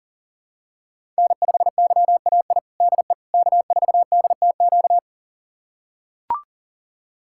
Morse code, heard as a keyed tone, about 40 wpm, 700 Hz. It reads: N5YAI DE K4DTQ
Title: USA Callsign DE Callsign 40wpm